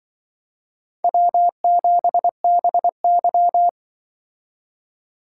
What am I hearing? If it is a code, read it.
Morse code, heard as a keyed tone, about 24 wpm, 690 Hz. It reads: W7BY
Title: Speed Racing - USA 1x2 Call Signs 20wpm